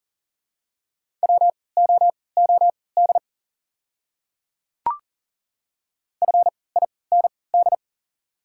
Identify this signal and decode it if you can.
Morse code, heard as a keyed tone, about 40 wpm, spaced as for 23 wpm, 700 Hz. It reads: WOOD FIND
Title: 3x Character Spacing - Top 300 Words - 40wpm